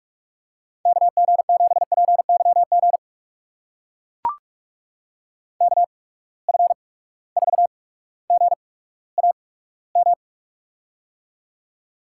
Morse code, heard as a keyed tone, about 43 wpm, spaced as for 19 wpm, 700 Hz. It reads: KG7PYG KF4GAM
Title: USA Call Signs - 6 Characters - 8x Character Spacing - 45wpm